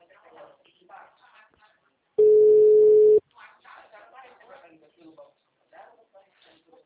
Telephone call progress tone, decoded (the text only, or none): ringback tone